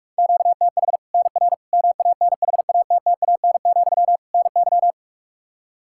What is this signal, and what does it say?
Morse code, heard as a keyed tone, about 46 wpm, 700 Hz. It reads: QTH NR MANHATTAN, NY
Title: Practice QSO - Exchange 1 - Beginning - 45wpm